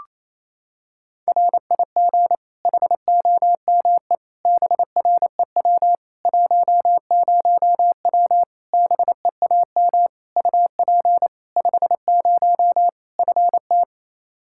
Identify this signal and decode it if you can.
Morse code, heard as a keyed tone, about 28 wpm, 700 Hz. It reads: RIG HOME BREW 10W BEAM UP 50 FT
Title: Practice QSO - Exchange 2 - Info - 28wpm